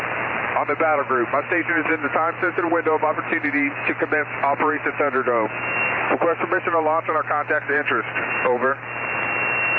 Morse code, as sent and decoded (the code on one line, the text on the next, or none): none